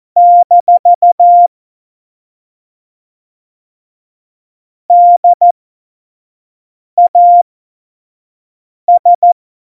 Morse code, heard as a keyed tone, about 14 wpm, spaced as for 5 wpm, 700 Hz. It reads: - DAS
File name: Morsekurs / Witze / Witz 22. August 17, Tempo 25 F70